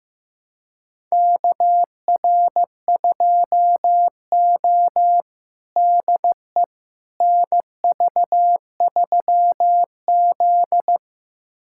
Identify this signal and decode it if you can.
Morse code, heard as a keyed tone, about 15 wpm, 700 Hz. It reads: KR2O DE NV3Z
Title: USA Callsign DE Callsign - 4 Characters - 15wpm